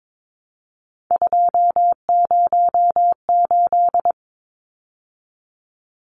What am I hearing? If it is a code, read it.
Morse code, heard as a keyed tone, about 22 wpm, 700 Hz. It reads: 208
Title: Numbers - 3-Digit 22wpm